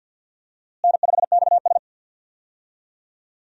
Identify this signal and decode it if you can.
Morse code, heard as a keyed tone, about 47 wpm, 700 Hz. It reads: N5XS
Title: USA Call Signs - 4 Characters - Repeated 2x - 50wpm